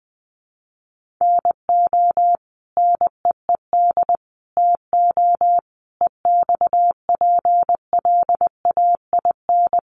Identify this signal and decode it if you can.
Morse code, heard as a keyed tone, about 20 wpm, 700 Hz. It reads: NO NEED TO EXPLAIN
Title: Sentences from Top 1100 Words 20wpm